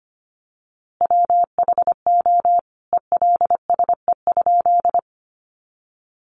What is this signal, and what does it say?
Morse code, heard as a keyed tone, about 25 wpm, 700 Hz. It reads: WHO ELSE?
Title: A Sound Of Thunder 25wpm